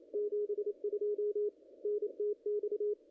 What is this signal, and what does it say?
Morse code, heard as a keyed tone, about 30 wpm, 410 Hz, with radio noise. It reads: 72 NTX
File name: contests / ss-cw-k5tr-op-n5rz -0100